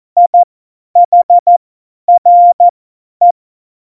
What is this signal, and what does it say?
Morse code, heard as a keyed tone, about 14 wpm, spaced as for 10 wpm, 700 Hz. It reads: IHRE
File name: Morsekurs / Witze / Witz 12. Januar 2021, 50 F70